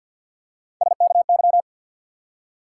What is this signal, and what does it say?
Morse code, heard as a keyed tone, about 53 wpm, 700 Hz. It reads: SKY
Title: Three Letter Words - 50wpm